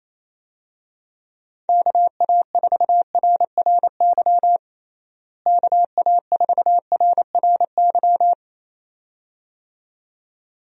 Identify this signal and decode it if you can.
Morse code, heard as a keyed tone, about 28 wpm, 700 Hz. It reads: KA4RRY KA4RRY
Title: USA Call Signs - 6 Characters - Repeated 2x - 28wpm